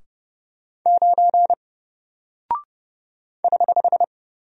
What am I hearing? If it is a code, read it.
Morse code, heard as a keyed tone, about 30 wpm, 700 Hz. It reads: 9 <HH>
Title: Single Character (Letter:Number:Punctuation:Prosign) Alt 30wpm